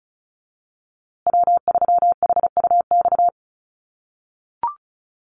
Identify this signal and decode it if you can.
Morse code, heard as a keyed tone, about 34 wpm, 700 Hz. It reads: W3HUX
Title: USA Call Signs Repeated 2x 35wpm